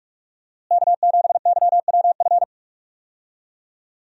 Morse code, heard as a keyed tone, about 46 wpm, 700 Hz. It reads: K7YWF
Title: USA Call Signs - 5 Characters - Repeated 2x - 45wpm